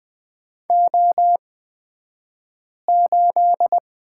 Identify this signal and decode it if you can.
Morse code, heard as a keyed tone, about 20 wpm, spaced as for 5 wpm, 700 Hz. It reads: O8
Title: 223 Lesson 36d - BT - Two Characters 20wpm